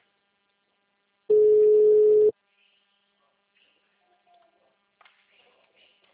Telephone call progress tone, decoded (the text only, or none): ringback tone